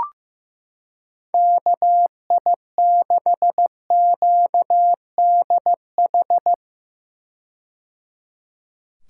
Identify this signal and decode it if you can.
Morse code, heard as a keyed tone, about 15 wpm, 700 Hz. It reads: KI6QDH
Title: K1USN SST QSO Practice - 15wpm